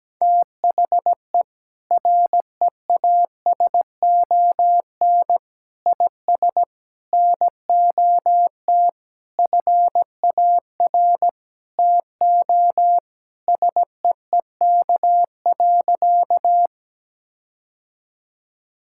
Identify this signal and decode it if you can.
Morse code, heard as a keyed tone, about 17 wpm, 700 Hz. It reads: THE REASON IS NOT FAR TO SEEK.
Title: Sentences from Top 1600 Words 17wpm